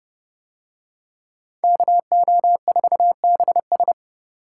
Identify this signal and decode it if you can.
Morse code, heard as a keyed tone, about 29 wpm, 700 Hz. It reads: KO4BS